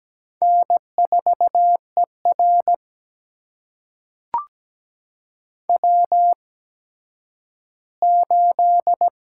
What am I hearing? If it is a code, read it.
Morse code, heard as a keyed tone, about 17 wpm, spaced as for 10 wpm, 700 Hz. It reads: N4ER W8
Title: USA Call Signs - 8x Character Spacing - 17wpm